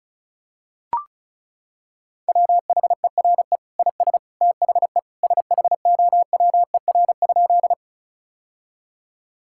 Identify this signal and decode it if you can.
Morse code, heard as a keyed tone, about 36 wpm, 700 Hz. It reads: WHERE IS THE SHOWER?
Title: Sentences from Top 1600 Words 35wpm